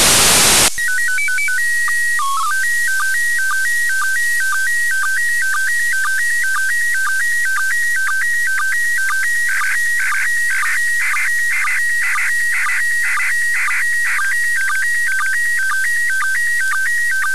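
SSTV (slow-scan television), PD 120.
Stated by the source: SSTV images sent from the ISS in memory of NASA astronaut Owen Garriott.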